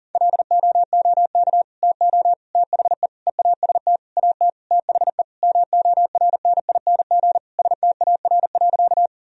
Morse code, heard as a keyed tone, about 40 wpm, 700 Hz. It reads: LOOK TO THE EAST AT THE MORNING STAR.
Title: Sentences from Top 1000 Words - Review Part 3 - 40wpm